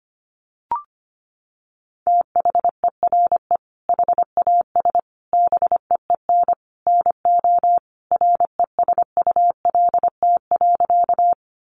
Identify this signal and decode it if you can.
Morse code, heard as a keyed tone, about 25 wpm, 700 Hz. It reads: THERE HAS BEEN NO RESULT.
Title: Sentences from Top 700 Words 25wpm